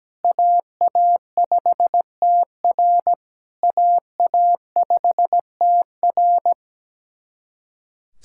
Morse code, heard as a keyed tone, about 17 wpm, 700 Hz. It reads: AA5TR AA5TR
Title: SOTA QSO Practice 17wpm